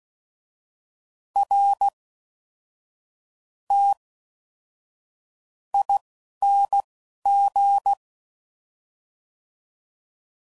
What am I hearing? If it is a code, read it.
Morse code, heard as a keyed tone, about 16 wpm, spaced as for 11 wpm, 800 Hz. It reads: R T ING
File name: Suffixes / Special INGs / Fast INGs